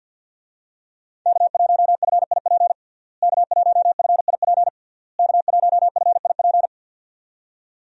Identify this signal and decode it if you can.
Morse code, heard as a keyed tone, about 52 wpm, 700 Hz. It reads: K1FIP K1FIP K1FIP
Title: USA Call Signs - 5 Characters - Repeated 3x - 50wpm